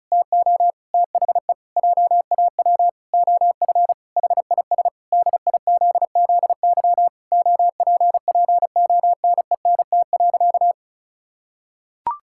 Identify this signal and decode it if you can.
Morse code, heard as a keyed tone, about 36 wpm, 700 Hz. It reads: TO THE JAW OF HIS DIZZY OPPONENT.
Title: Pangram Challenge v2 - 35wpm